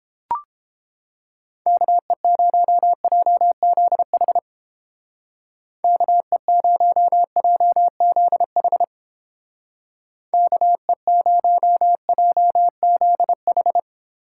Morse code, heard as a keyed tone, about 29 wpm, 700 Hz. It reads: KE0JZH KE0JZH KE0JZH